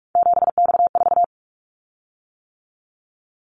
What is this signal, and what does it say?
Morse code, heard as a keyed tone, about 44 wpm, 700 Hz. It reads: 7X4
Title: Callsign Prefixes for Top 100 DXCC - 45wpm